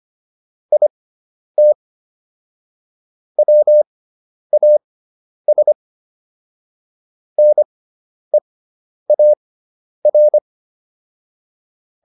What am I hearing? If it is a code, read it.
Morse code, heard as a keyed tone, about 25 wpm, spaced as for 10 wpm, 600 Hz. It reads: IT WAS NEAR